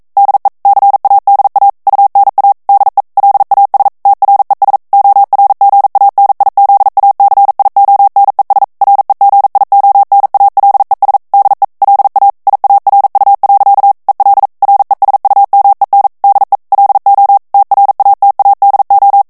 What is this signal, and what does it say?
Morse code, heard as a keyed tone, about 42 wpm, 800 Hz. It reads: DE CADA UNA DE LAS TRES ORGANIZACIONES REGIONALES DE LA IARU. EL RESUMEN DE LO TRATADO
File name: morse mp3 / ESPAGNOL TEXTE EN CLAIR 1 VITESSE 40 MOTS MINUTES